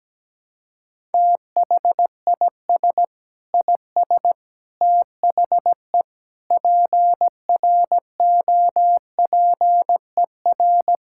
Morse code, heard as a keyed tone, about 17 wpm, 700 Hz. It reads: THIS IS THE PROPER
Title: Sentences from Top 1000 Words - Review Part 5 - 17wpm